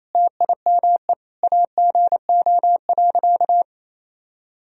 Morse code, heard as a keyed tone, about 28 wpm, 700 Hz. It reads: TIME AGO.
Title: Sentences from Top 1000 Words - Review Part 3 - 28wpm